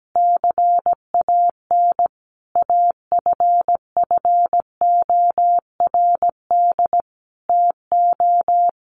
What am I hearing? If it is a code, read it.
Morse code, heard as a keyed tone, about 17 wpm, 700 Hz. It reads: CAN AFFORD TO
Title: Sentences from Top 1500 Words 17wpm